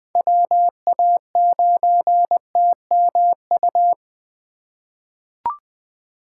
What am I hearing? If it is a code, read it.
Morse code, heard as a keyed tone, about 20 wpm, 700 Hz. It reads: WA9TMU
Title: 2023 POTA Hunter Call Signs - 20wpm